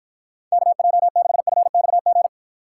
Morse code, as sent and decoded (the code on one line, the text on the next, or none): -.- .--- -.... ..-. -..- --.
KJ6FXG